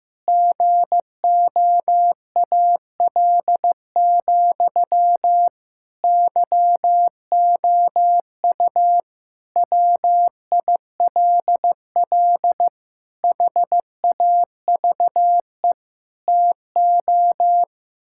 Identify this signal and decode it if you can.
Morse code, heard as a keyed tone, about 15 wpm, 700 Hz. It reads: GOAL, YOU WILL HAVE TO
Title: Sentences from Top 1000 Words - Review Part 6 - 15wpm